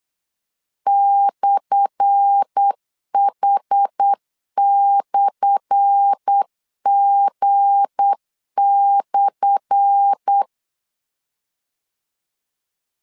Morse code, heard as a keyed tone, about 8 wpm, 800 Hz. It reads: /H/G/
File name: LEZIONI / 10 SIGNS / 10A SLASH 8wpm 3452